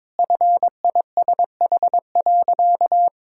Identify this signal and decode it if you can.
Morse code, heard as a keyed tone, about 22 wpm, 690 Hz. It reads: FISH.
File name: practice / 22wpm / Easy Sentences with Hints - 22wpm